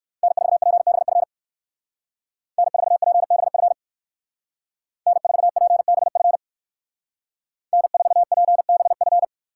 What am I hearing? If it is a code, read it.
Morse code, heard as a keyed tone, about 67 wpm, 700 Hz. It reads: N4PBF N4PBF N4PBF N4PBF